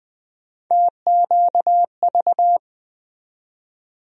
Morse code, heard as a keyed tone, about 20 wpm, 700 Hz. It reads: TQV
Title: USA Call Sign Postfixes - 3 Letters 20wpm